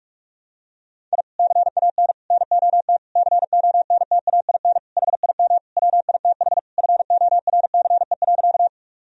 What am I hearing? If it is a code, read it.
Morse code, heard as a keyed tone, about 41 wpm, 700 Hz. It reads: I CAN NOT CONTAIN HIM WITH FORCE.
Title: Sentences from Top 1000 Words - Review Part 3 - 45wpm